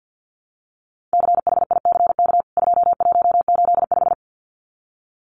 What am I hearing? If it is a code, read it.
Morse code, heard as a keyed tone, about 50 wpm, 700 Hz. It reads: CHICK 2185
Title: CWT Exchange 50wpm